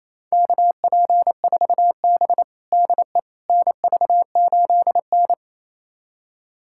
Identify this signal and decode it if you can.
Morse code, heard as a keyed tone, about 28 wpm, 700 Hz. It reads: KP4B DE NV8N
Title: USA Callsign DE Callsign - 4 Characters - 28wpm